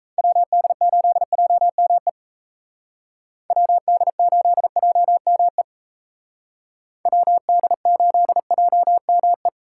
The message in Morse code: WD8JME WD8JME WD8JME

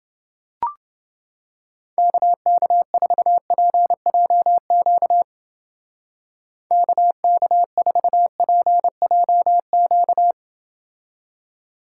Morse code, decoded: KK4PJQ KK4PJQ

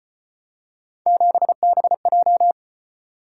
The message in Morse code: 7BJ

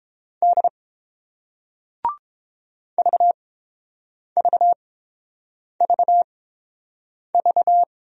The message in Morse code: D VVVV

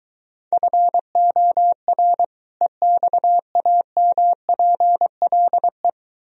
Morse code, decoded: FOR EXAMPLE

